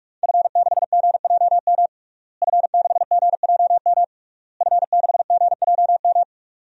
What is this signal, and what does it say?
Morse code, heard as a keyed tone, about 46 wpm, 700 Hz. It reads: F6GJM F6GJM F6GJM
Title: DXCC-100 Call Signs - 5 Characters - Repeated 3x - 45wpm